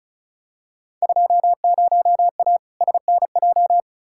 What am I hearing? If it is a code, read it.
Morse code, heard as a keyed tone, about 35 wpm, 700 Hz. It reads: 20A SNJ
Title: Field Day Exchanges Encore - 35wpm